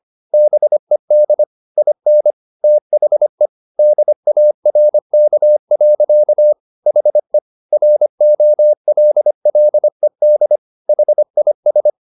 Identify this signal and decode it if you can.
Morse code, heard as a keyed tone, about 25 wpm, 600 Hz. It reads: BED IN THE DARK. HE ROLLED HIS
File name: the old man and the sea part 1 25wpm